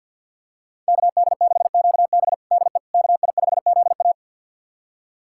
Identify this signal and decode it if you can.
Morse code, heard as a keyed tone, about 50 wpm, 700 Hz. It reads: KD6QB DE KI5ZA
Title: USA Callsign DE Callsign - 5 Characters - 50wpm